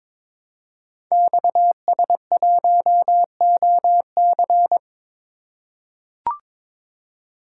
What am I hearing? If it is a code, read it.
Morse code, heard as a keyed tone, about 22 wpm, 700 Hz. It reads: XS1OC
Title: Code Groups - 5 Letter:Numbers - 22wpm